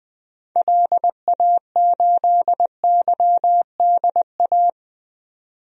Morse code, heard as a keyed tone, about 20 wpm, 700 Hz. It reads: LA8YDA
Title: DXCC-100 Call Signs - 6 Characters - 20wpm